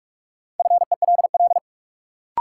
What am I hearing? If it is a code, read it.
Morse code, heard as a keyed tone, about 46 wpm, 700 Hz. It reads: FELL
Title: Sets of 5 Words Encore - New Word Drill - 45wpm